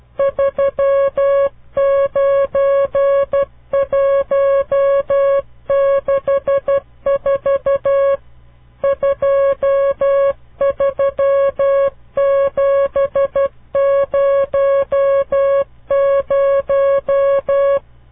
Morse code, heard as a keyed tone, about 12 wpm, 550 Hz. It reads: 39164 23700
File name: CW Learning Programs / farnsworth / unit 5